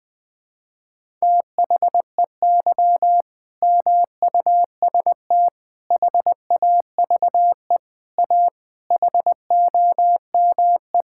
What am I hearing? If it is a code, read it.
Morse code, heard as a keyed tone, about 20 wpm, 700 Hz. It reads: THEY MUST HAVE A HOME
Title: Sentences from Top 1000 Words - Review Part 5 - 20wpm